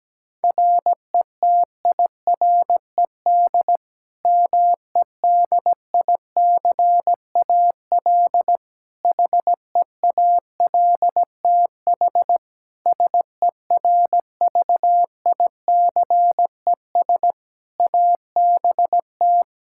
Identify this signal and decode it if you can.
Morse code, heard as a keyed tone, about 17 wpm, 700 Hz. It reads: RETIRED MEDICAL HEALTH SERVICES ABT
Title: Practice QSO - Exchange 3 - More Info - 17wpm